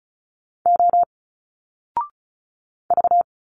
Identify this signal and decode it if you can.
Morse code, heard as a keyed tone, about 36 wpm, 700 Hz. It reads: O V
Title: Single Letters 35wpm